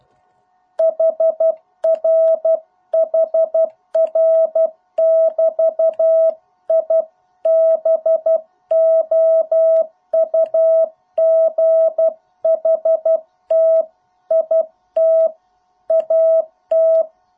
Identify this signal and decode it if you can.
Morse code, heard as a keyed tone, about 12 wpm, 650 Hz. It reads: HR HR <BT> I BOUGHT IT AT